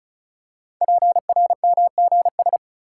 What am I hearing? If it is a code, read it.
Morse code, heard as a keyed tone, about 36 wpm, 690 Hz. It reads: PRMGS
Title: Code Groups - 5 Letters - 35wpm